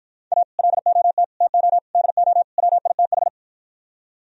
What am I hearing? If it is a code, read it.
Morse code, heard as a keyed tone, about 54 wpm, 700 Hz. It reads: A LOT TO DO WITH